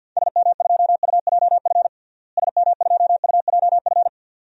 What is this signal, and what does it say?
Morse code, heard as a keyed tone, about 50 wpm, 700 Hz. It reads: SM2UJF SM2UJF
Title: DXCC-100 Call Signs - 6 Characters - Repeated 2x - 50wpm